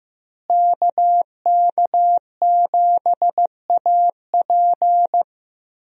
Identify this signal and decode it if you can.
Morse code, heard as a keyed tone, about 16 wpm, 700 Hz. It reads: KK7AP
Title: USA Call Signs - 5 Characters - Repeated 3x - 15wpm